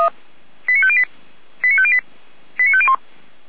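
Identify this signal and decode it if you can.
Five-tone selective call (ZVEI), heard as quick runs of five tones, tones of about 70 ms. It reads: 89508 89508 89501
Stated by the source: Five-tone ZVEI selective call 89099, received on an SDR.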